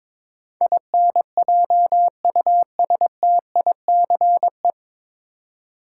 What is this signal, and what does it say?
Morse code, heard as a keyed tone, about 22 wpm, 700 Hz. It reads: INJUSTICE
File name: 9 Letter Words - First 3 Letters Given - 22wpm